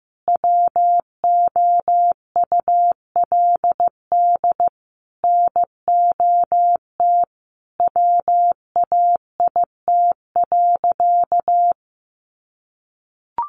Morse code, heard as a keyed tone, about 15 wpm, 700 Hz. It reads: WOULD NOT WAIT.
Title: Sentences from Top 1400 Words 15wpm